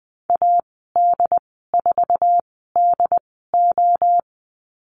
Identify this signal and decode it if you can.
Morse code, heard as a keyed tone, about 20 wpm, spaced as for 15 wpm, 700 Hz. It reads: AD4DO
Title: 069 Lesson 12g - Letter d - Call Signs 20wpm